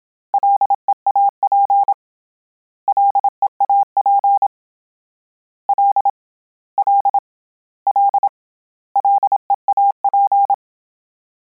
Morse code, heard as a keyed tone, about 27 wpm, 800 Hz. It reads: LEAP LEAP L L L LEAP